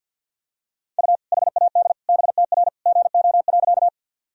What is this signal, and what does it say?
Morse code, heard as a keyed tone, about 50 wpm, 700 Hz. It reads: U HAD BTR GO.
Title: Sentences from Top 400 Words and Abbreviations 50wpm